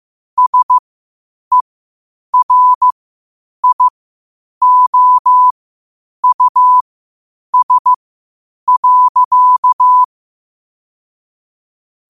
Morse code, heard as a keyed tone, about 15 wpm, spaced as for 8 wpm, 1000 Hz. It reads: SERIOUS.
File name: qotdfiles / 8wpm